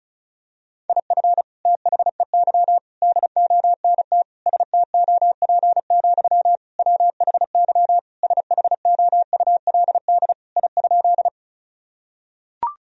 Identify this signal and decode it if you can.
Morse code, heard as a keyed tone, about 34 wpm, 700 Hz. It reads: IF THEY DONT STOP, WHY SHOULD I?
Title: Sentences from Top 1000 Words - Review Part 1 - 35wpm